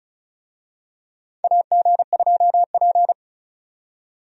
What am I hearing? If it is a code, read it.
Morse code, heard as a keyed tone, about 34 wpm, 700 Hz. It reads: AG2P
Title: USA Call Signs - 4 Characters - 35wpm